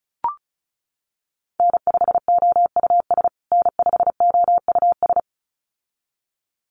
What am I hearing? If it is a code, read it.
Morse code, heard as a keyed tone, about 34 wpm, 700 Hz. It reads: N5OUS N5OUS
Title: USA Call Signs Repeated 2x 35wpm